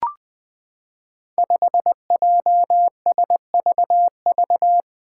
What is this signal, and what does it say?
Morse code, heard as a keyed tone, about 20 wpm, 700 Hz. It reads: 5JSVV
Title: Five Challenging Letter-Number Combinations 20wpm